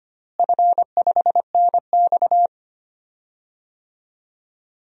Morse code, heard as a keyed tone, about 25 wpm, 700 Hz. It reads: F5NX
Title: DXCC-100 Call Signs - 4 Characters - Repeated 2x - 25wpm